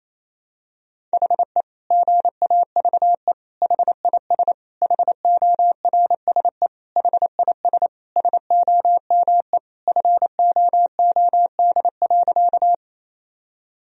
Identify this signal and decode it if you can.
Morse code, heard as a keyed tone, about 28 wpm, 700 Hz. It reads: HE GAVE HIS HORSE HIS SOME FOOD.
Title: Sentences from Top 1000 Words - Review Part 1 - 28wpm